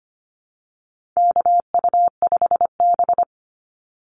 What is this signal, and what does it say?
Morse code, heard as a keyed tone, about 25 wpm, 700 Hz. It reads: KU5B
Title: Call Signs - 25wpm